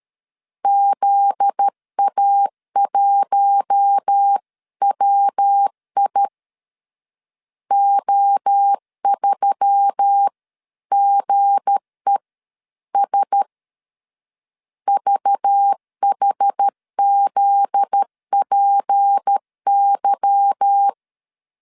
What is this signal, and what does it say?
Morse code, heard as a keyed tone, about 13 wpm, 800 Hz. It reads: ZA1WI O3GES VHZPY
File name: LEZIONI / 8 YJ45 / 8 Y 12wpm 2228